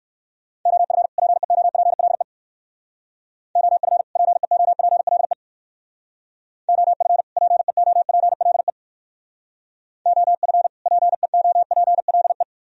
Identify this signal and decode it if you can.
Morse code, heard as a keyed tone, about 57 wpm, 700 Hz. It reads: OF PEOPLE OF PEOPLE OF PEOPLE OF PEOPLE